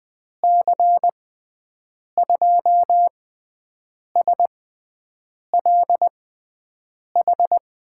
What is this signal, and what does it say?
Morse code, heard as a keyed tone, about 20 wpm, spaced as for 7 wpm, 700 Hz. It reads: C2SLH
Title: USA Call Signs - 6 Characters - 6x Character Spacing - 20wpm